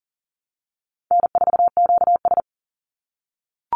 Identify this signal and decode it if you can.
Morse code, heard as a keyed tone, about 40 wpm, 700 Hz. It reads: N4QS
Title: Easy Call Signs - 40wpm